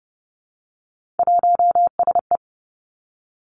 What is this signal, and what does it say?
Morse code, heard as a keyed tone, about 30 wpm, 700 Hz. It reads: 1SE
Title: Three Challenging Letter-Number Combinations 30wpm